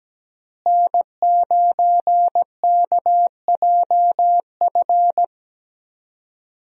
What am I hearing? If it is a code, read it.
Morse code, heard as a keyed tone, about 17 wpm, 700 Hz. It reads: N9KJF
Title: USA Call Signs - 5 Characters - Repeated 3x - 17wpm